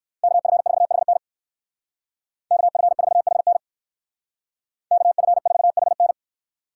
Morse code, heard as a keyed tone, about 58 wpm, 700 Hz. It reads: KF4HN KF4HN KF4HN